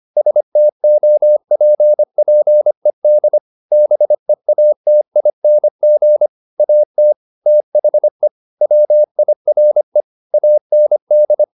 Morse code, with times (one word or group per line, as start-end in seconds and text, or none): 0.2-3.4 STOPPED
3.7-6.3 BEATING
6.6-7.1 AT
7.5-8.3 THE
8.6-10.0 WIRE
10.3-11.4 AND